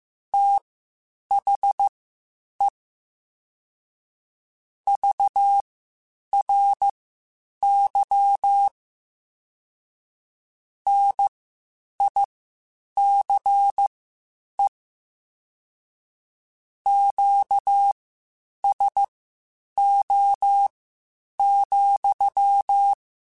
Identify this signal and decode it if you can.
Morse code, heard as a keyed tone, about 15 wpm, spaced as for 8 wpm, 800 Hz. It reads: THE VRY NICE QSO,